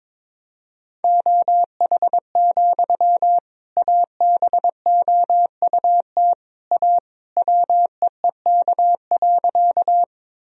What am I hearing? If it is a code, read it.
Morse code, heard as a keyed tone, about 22 wpm, 700 Hz. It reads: OH, ABOUT A WEEK.
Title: Sentences from Top 1000 Words - Review Part 2 - 22wpm